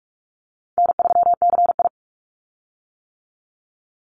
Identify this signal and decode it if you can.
Morse code, heard as a keyed tone, about 46 wpm, 700 Hz. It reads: N3CI
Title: Easy Call Signs - 45wpm